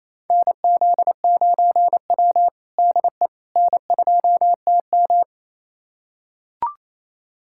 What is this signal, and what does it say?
Morse code, heard as a keyed tone, about 28 wpm, 700 Hz. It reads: NZ9W DE N2TM
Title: USA Callsign DE Callsign - 4 Characters - 28wpm